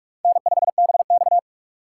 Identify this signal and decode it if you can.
Morse code, heard as a keyed tone, about 46 wpm, 700 Hz. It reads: N5BX
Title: USA Call Signs - 4 Characters - 3x Character Spacing - 45wpm